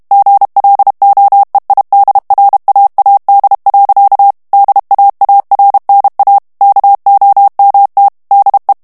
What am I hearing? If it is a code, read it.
Morse code, heard as a keyed tone, about 32 wpm, 800 Hz. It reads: GLOEIDRAAD. DAARNA KOMT DE
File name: HOLLANDAIS texte clair 2 vitesse 30 mots minute